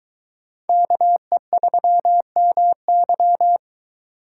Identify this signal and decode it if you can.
Morse code, heard as a keyed tone, about 23 wpm, 700 Hz. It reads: KE3MY